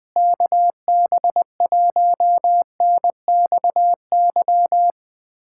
Morse code, heard as a keyed tone, about 20 wpm, 700 Hz. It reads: KB1NXY
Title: USA Callsign DE Callsign - 6 Characters - 20wpm